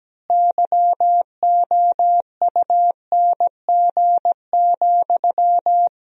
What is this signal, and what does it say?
Morse code, heard as a keyed tone, about 17 wpm, 700 Hz. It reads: YOUNG,
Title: 3x Word-Spacing - Sentences from Top 400 Words - 17wpm